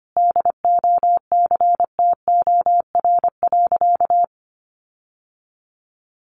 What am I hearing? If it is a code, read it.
Morse code, heard as a keyed tone, about 25 wpm, 700 Hz. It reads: DOCTOR.
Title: Sentences from Top 1500 Words 25wpm